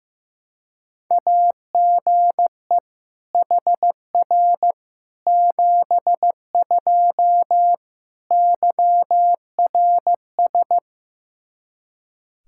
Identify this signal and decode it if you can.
Morse code, heard as a keyed tone, about 16 wpm, 700 Hz. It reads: AGE HR 72 YRS
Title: Practice QSO - Exchange 3 - More Info - 15wpm